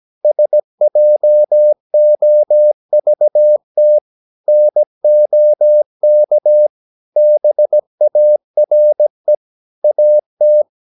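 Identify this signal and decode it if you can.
Morse code, heard as a keyed tone, about 17 wpm, 600 Hz. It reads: SJOVT NOK BARE AT
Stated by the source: Slow (QRS) Morse practice at 17 wpm, from CW med Gnister.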